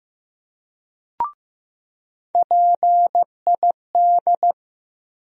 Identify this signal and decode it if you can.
Morse code, heard as a keyed tone, about 16 wpm, 700 Hz. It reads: PID
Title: USA Call Sign Postfixes - 3 Letters 15wpm